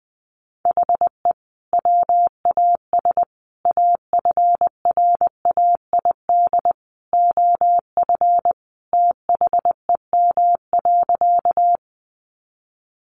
Morse code, heard as a keyed tone, about 20 wpm, 700 Hz. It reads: HE WAS AFRAID OF THEM.
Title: Sentences from Top 1500 Words 20wpm